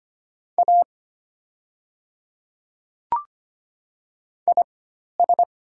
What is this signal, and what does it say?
Morse code, heard as a keyed tone, about 25 wpm, spaced as for 12 wpm, 700 Hz. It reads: A IS
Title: 4x Character Spacing - Top 100 Words - 25wpm